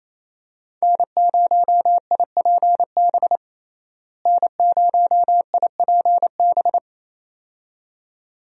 Morse code, heard as a keyed tone, about 28 wpm, 700 Hz. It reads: N0IPB N0IPB
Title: USA Call Signs - 5 Characters - Repeated 2x - 28wpm